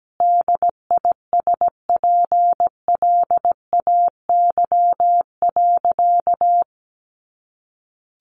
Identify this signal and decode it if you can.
Morse code, heard as a keyed tone, about 17 wpm, 700 Hz. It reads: DISPLAY.
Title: Sentences from Top 1200 Words 17wpm